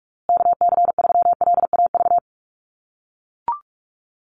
Morse code, heard as a keyed tone, about 44 wpm, 700 Hz. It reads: KC3LAV
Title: USA Call Signs Repeated 3x 45wpm